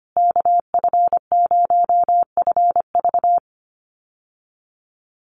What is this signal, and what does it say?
Morse code, heard as a keyed tone, about 25 wpm, 700 Hz. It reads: KF0FV
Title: USA Call Signs Repeated 3x 25wpm